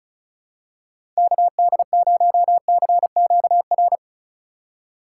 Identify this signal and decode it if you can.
Morse code, heard as a keyed tone, about 34 wpm, 700 Hz. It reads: KD0CQR
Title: USA Call Signs - 6 Characters - 4x Character Spacing - 35wpm